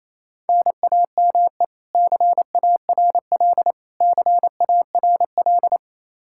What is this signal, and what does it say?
Morse code, heard as a keyed tone, about 28 wpm, 700 Hz. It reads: NAME CARL CARL
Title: K1USN SST QSO Practice 28wpm